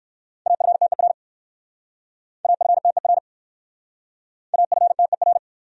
Morse code, heard as a keyed tone, about 62 wpm, 700 Hz. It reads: AFTER AFTER AFTER